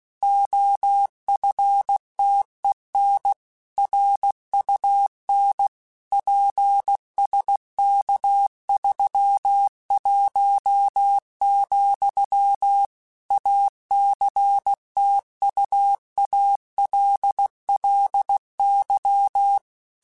Morse code, heard as a keyed tone, about 16 wpm, 800 Hz. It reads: OFTEN RUN PSK31, ACTUALLY
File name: QSOs / QSOs-mp3s / kc7haa@15